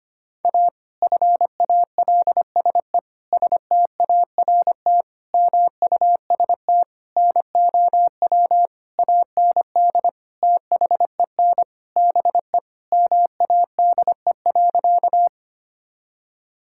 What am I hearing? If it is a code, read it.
Morse code, heard as a keyed tone, about 25 wpm, 700 Hz. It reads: A FALSE START MUST NOW AND THEN BE MADE.
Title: Sentences from Top 1600 Words 25wpm